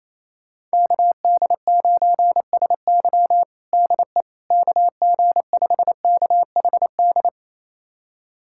Morse code, heard as a keyed tone, about 28 wpm, 700 Hz. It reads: KD9SY DE KG5KHD
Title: USA Callsign DE Callsign 28wpm